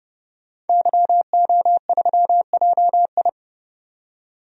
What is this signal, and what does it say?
Morse code, heard as a keyed tone, about 31 wpm, 700 Hz. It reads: YO3JI
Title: DXCC-100 Call Signs - 5 Characters - 30wpm